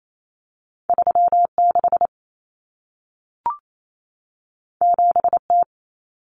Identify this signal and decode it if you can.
Morse code, heard as a keyed tone, about 28 wpm, 700 Hz. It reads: 36 7T
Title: Common Cut Numbers - 2-Digit 28wpm